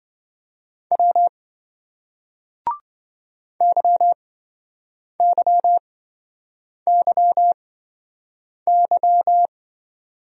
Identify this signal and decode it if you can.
Morse code, heard as a keyed tone, about 29 wpm, spaced as for 7 wpm, 700 Hz. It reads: W YYYY